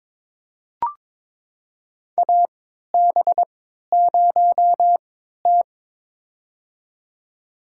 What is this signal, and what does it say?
Morse code, heard as a keyed tone, about 22 wpm, spaced as for 12 wpm, 700 Hz. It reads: AB0T
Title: USA Call Signs - 4 Characters - 3x Character Spacing - 22wpm